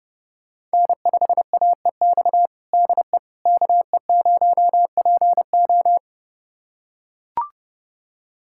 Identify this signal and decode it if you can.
Morse code, heard as a keyed tone, about 30 wpm, 700 Hz. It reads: N5AEX DE KE0PO
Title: USA Callsign DE Callsign - 5 Characters - 30wpm